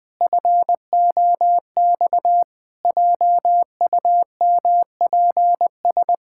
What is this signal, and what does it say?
Morse code, heard as a keyed tone, about 20 wpm, 700 Hz. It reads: FOX JUMPS
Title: Pangram Challenge v2 - 20wpm